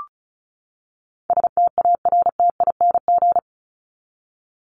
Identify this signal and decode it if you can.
Morse code, heard as a keyed tone, about 34 wpm, 700 Hz. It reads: STARTING
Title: ING Suffix - Top 100 Words with Base Words 35wpm